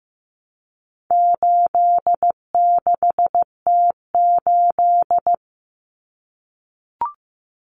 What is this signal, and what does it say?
Morse code, heard as a keyed tone, about 15 wpm, 700 Hz. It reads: 86T8
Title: Common Cut Numbers - 4-Digit 15wpm